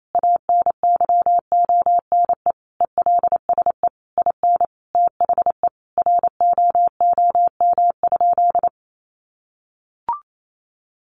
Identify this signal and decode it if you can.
Morse code, heard as a keyed tone, about 28 wpm, 700 Hz. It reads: ANYONE ELSE IN THE ROOM?
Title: Sentences from Top 900 Words 28wpm